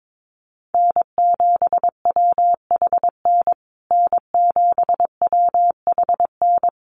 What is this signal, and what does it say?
Morse code, heard as a keyed tone, about 22 wpm, 700 Hz. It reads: N7WHN N7WHN
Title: USA Call Signs Repeated 2x 22wpm